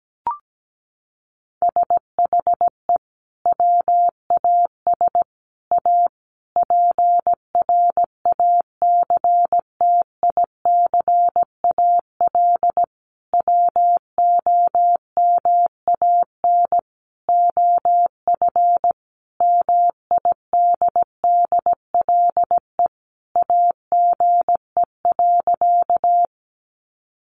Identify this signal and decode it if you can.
Morse code, heard as a keyed tone, about 17 wpm, 700 Hz. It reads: SHE WAS A PRACTICAL WOMAN OF MIDDLE AGE.
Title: Sentences from Top 1400 Words 17wpm